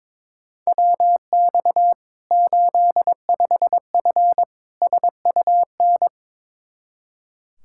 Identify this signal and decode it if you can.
Morse code, heard as a keyed tone, about 22 wpm, 700 Hz. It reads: WX 85F SUN
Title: Practice QSO - Exchange 2 - Info - 22wpm